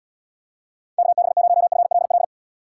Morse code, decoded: KB0FLL

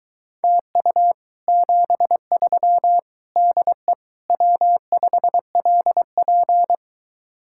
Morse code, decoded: TU 73 DE W5LP

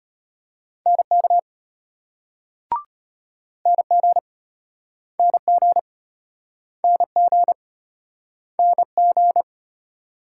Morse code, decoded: NK NG NG NG NG

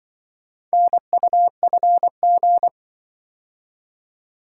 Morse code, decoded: NUFG

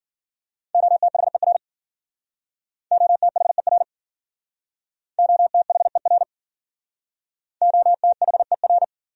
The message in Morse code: OTHER OTHER OTHER OTHER

